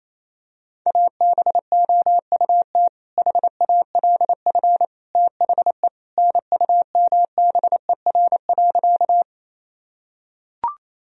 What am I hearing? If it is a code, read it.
Morse code, heard as a keyed tone, about 28 wpm, 700 Hz. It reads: ABOUT HALF THE NUMBER.
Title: Sentences from Top 1000 Words - Review Part 1 - 28wpm